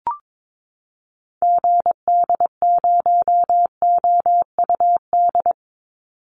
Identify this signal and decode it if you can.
Morse code, heard as a keyed tone, about 22 wpm, 700 Hz. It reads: GD0OUD
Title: Call Signs Repeated 2x - 22wpm